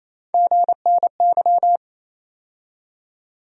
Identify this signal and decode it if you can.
Morse code, heard as a keyed tone, about 28 wpm, 700 Hz. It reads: GNY
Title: USA Call Sign Postfixes - 3 Letters 28wpm